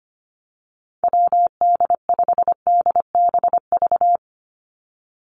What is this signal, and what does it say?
Morse code, heard as a keyed tone, about 25 wpm, 700 Hz. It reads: WD5DBV
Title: Call Signs - 25wpm